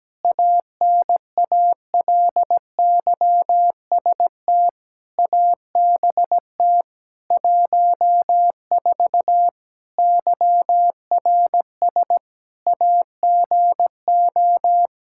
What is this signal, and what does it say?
Morse code, heard as a keyed tone, about 17 wpm, 700 Hz. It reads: ANALYST ABT 14 YRS AGO
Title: Practice QSO - Exchange 3 - More Info - 17wpm